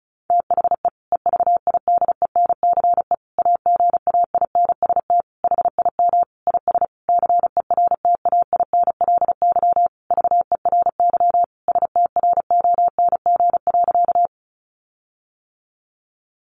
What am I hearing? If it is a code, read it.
Morse code, heard as a keyed tone, about 34 wpm, 700 Hz. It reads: THE EVIDENCE AGAINST HIM IS CERTAINLY VERY STRONG.
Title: Sentences from Top 1300 Words 35wpm